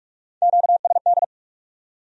Morse code, heard as a keyed tone, about 61 wpm, spaced as for 52 wpm, 700 Hz. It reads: QSD
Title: Speed-Racing Q-Codes - 45wpm